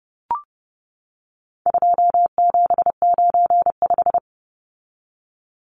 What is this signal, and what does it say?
Morse code, heard as a keyed tone, about 30 wpm, 700 Hz. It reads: 2795
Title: Numbers - 4-Digit 30wpm